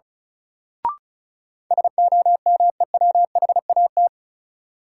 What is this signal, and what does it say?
Morse code, heard as a keyed tone, about 34 wpm, 700 Hz. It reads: SOMEWHAT
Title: Sentences from Top 1000 Words - Review Part 1 - New Word Drill - 35wpm